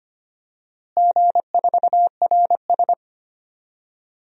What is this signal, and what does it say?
Morse code, heard as a keyed tone, about 25 wpm, 700 Hz. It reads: G4RS
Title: DXCC-100 Call Signs - 4 Characters - Repeated 2x - 25wpm